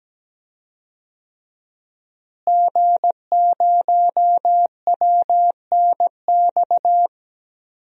Morse code, heard as a keyed tone, about 17 wpm, 700 Hz. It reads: G0WNX
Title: DXCC-100 Call Signs - 5 Characters - Repeated 3x - 17wpm